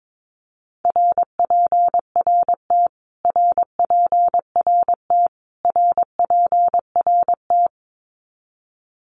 Morse code, heard as a keyed tone, about 22 wpm, 700 Hz. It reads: RPRT RPRT RPRT
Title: Abbreviations Repeated 3x - 22wpm